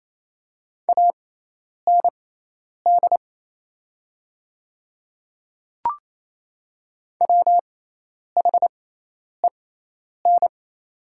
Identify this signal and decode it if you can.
Morse code, heard as a keyed tone, about 28 wpm, spaced as for 10 wpm, 700 Hz. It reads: AND WHEN
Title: 6x Character Spacing - Top 200 Words - 28wpm